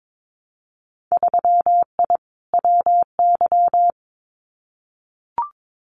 Morse code, heard as a keyed tone, about 22 wpm, 700 Hz. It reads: 3I WY
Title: Winter Field Day Exchanges Encore 22wpm